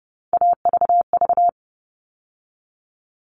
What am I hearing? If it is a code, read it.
Morse code, heard as a keyed tone, about 30 wpm, 700 Hz. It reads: AVV
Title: All Cut Numbers - 3-Digit 30wpm